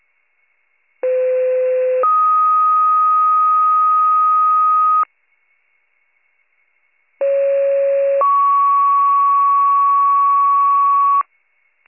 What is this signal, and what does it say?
Two-tone page, heard as a one-second tone, then a three-second tone, about 510 then 1250 Hz, about 550 then 1120 Hz. It is heard from 1.0 to 11.2 s.